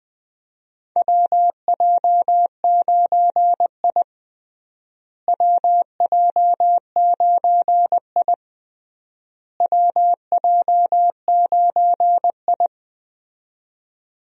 Morse code, heard as a keyed tone, about 20 wpm, 700 Hz. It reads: WJ9I WJ9I WJ9I
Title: USA Call Signs - 4 Characters - Repeated 3x - 20wpm